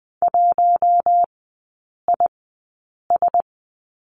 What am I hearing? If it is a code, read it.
Morse code, heard as a keyed tone, about 19 wpm, spaced as for 8 wpm, 690 Hz. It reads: 1IS
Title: 145 Lesson 24e - Letter y - Three Characters 20wpm